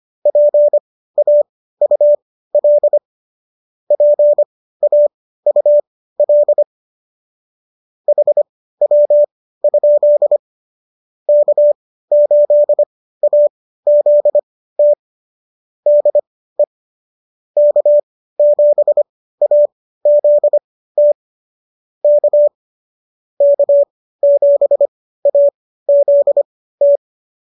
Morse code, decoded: PAUL PAUL HW? K8AZT DE K7AZT K K7AZT

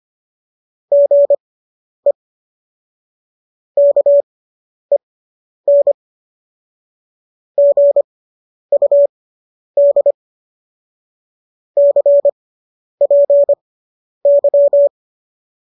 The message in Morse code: GE KEN GUD CPY

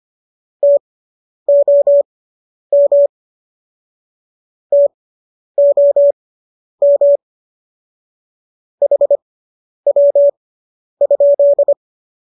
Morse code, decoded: TOM TOM HW?

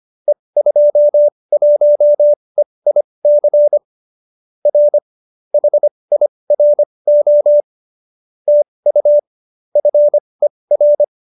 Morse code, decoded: E21EIC R HIRO TU FER

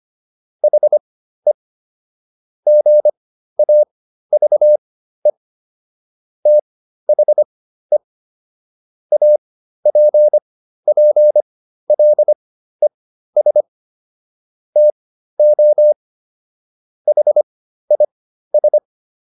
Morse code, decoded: HE GAVE THE APPLES TO HIS

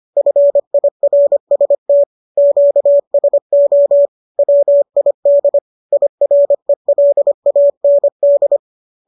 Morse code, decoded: FIRST QSO WID IRELAND

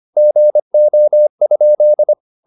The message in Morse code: GO?

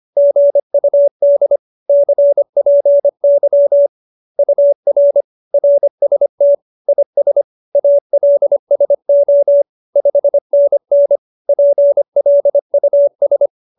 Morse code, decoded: GUD CPY UR RST IS ALSO 5NN PLUS

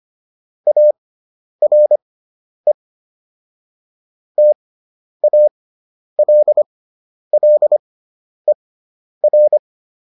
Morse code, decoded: ARE TALLER

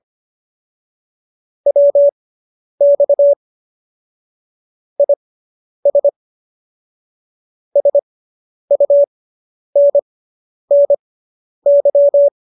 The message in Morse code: WX IS SUNNY

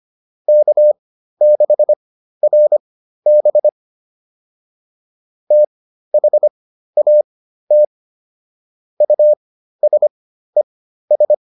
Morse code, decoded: K6RB THAT USES

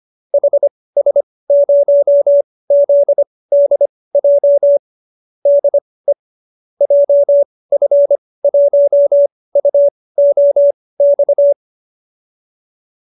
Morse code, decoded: HS0ZDJ DE JF1UOX

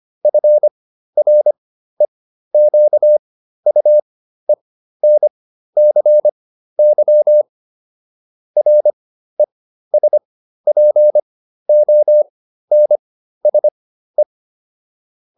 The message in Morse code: FREQUENCY RESPONSE

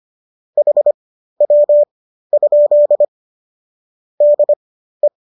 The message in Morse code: HW? DE

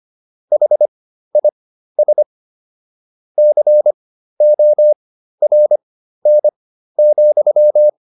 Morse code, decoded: HIS CORN,

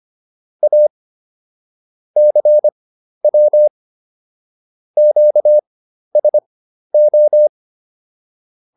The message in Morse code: A CW QSO

